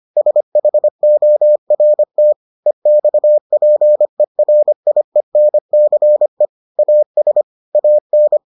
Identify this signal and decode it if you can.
Morse code, heard as a keyed tone, about 25 wpm, 600 Hz. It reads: SHORT EXPERIENCE AS AN